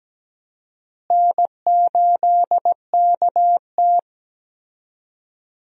Morse code, heard as a keyed tone, about 17 wpm, 700 Hz. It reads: N8KT
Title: USA Call Signs - 4 Characters - Repeated 2x - 17wpm